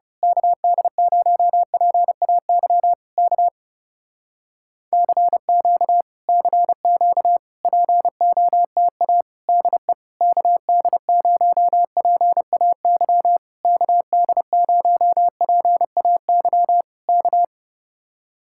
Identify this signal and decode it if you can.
Morse code, heard as a keyed tone, about 28 wpm, 700 Hz. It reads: KD0PAY K CQ CQ POTA DE KD0PAY KD0PAY K
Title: Speed-Racing Common QSO Phrases Encore - 30wpm